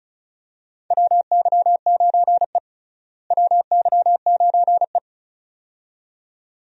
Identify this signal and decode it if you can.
Morse code, heard as a keyed tone, about 36 wpm, 700 Hz. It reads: WY9E WY9E
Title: USA Call Signs - 4 Characters - Repeated 2x - 35wpm